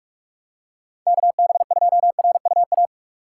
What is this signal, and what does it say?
Morse code, heard as a keyed tone, about 46 wpm, 700 Hz. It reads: KB2RUA
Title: USA Call Signs - 2x Character Spacing - 45wpm